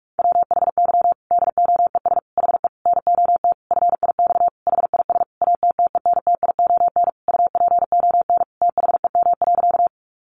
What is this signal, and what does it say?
Morse code, heard as a keyed tone, about 44 wpm, 700 Hz. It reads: WHY DOES HE NOT FIX HIS ATTENTION UPON THEM.
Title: Sentences from Top 1400 Words 45wpm